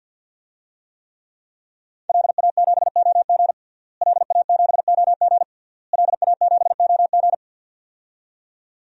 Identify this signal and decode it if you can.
Morse code, heard as a keyed tone, about 47 wpm, 700 Hz. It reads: LA7OG LA7OG LA7OG
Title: DXCC-100 Call Signs - 5 Characters - Repeated 3x - 50wpm